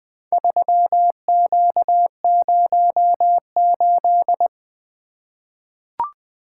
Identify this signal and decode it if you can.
Morse code, heard as a keyed tone, about 21 wpm, 700 Hz. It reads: 3Q08
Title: Four Letter-Number Combinations - 20wpm